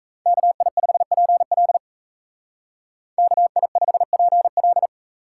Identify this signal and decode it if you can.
Morse code, heard as a keyed tone, about 33 wpm, 700 Hz. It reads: KI5PL KI5PL
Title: Speed Racing - USA 2x2 Call Signs 28wpm